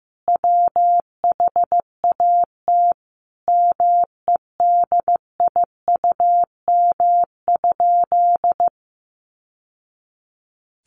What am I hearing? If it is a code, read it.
Morse code, heard as a keyed tone, about 15 wpm, 700 Hz. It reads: WHAT MEDIUM?
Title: Sentences from Top 900 Words 15wpm